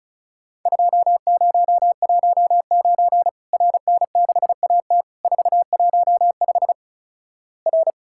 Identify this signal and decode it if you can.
Morse code, heard as a keyed tone, about 36 wpm, 700 Hz. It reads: 2019 RN6AT 415 R
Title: Worked All Europe DX Contest - QTC Exchange - 35wpm